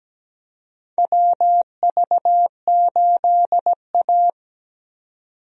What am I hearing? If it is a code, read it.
Morse code, heard as a keyed tone, about 17 wpm, 700 Hz. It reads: WV8A
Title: USA Call Signs - 4 Characters - Repeated 3x - 17wpm